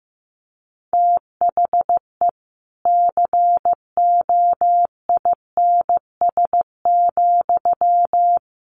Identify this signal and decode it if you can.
Morse code, heard as a keyed tone, about 15 wpm, 700 Hz. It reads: THE COINS,
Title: A Sound Of Thunder 15wpm